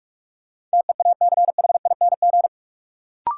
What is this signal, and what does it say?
Morse code, heard as a keyed tone, about 41 wpm, 700 Hz. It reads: TEACHING
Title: Sentences from Top 1100 Words - New Word Drill - 45wpm